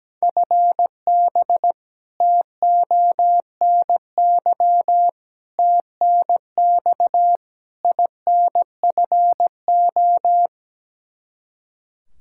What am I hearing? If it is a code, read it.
Morse code, heard as a keyed tone, about 17 wpm, 700 Hz. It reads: FB TONY TNX INFO
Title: Practice QSO - Exchange 3 - More Info - 17wpm